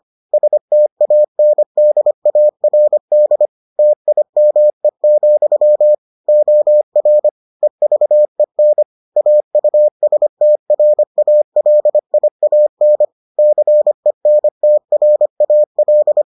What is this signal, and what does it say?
Morse code, heard as a keyed tone, about 25 wpm, 600 Hz. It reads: STANDARD TIME, OR EVEN AUSTRALIAN CENTRAL